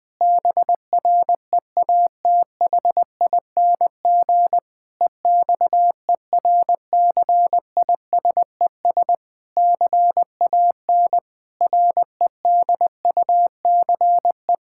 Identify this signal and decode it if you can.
Morse code, heard as a keyed tone, about 21 wpm, 700 Hz. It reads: BREATHING EXERCISES CAN REDUCE
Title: Difficult Sentences with Hints - 20wpm